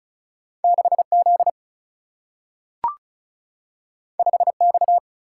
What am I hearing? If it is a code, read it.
Morse code, heard as a keyed tone, about 36 wpm, 700 Hz. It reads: 6Z 5X
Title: Single-Letter Single-Number 35wpm